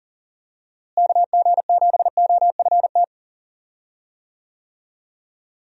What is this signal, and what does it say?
Morse code, heard as a keyed tone, about 40 wpm, 700 Hz. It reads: KG7OFT
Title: USA Call Signs - 6 Characters - Repeated 3x - 40wpm